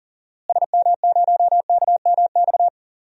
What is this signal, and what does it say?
Morse code, heard as a keyed tone, about 40 wpm, 700 Hz. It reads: SM0KMX
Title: DXCC-100 Call Signs - 6 Characters - Repeated 2x - 40wpm